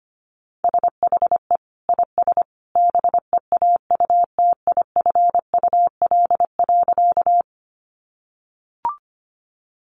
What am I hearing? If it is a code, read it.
Morse code, heard as a keyed tone, about 25 wpm, 700 Hz. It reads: SHE IS BEAUTIFUL.
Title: Sentences from Top 900 Words 25wpm